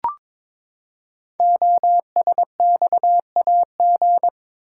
Morse code, heard as a keyed tone, about 22 wpm, 700 Hz. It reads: OSXAG
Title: Code Groups - 5 Letters - 22wpm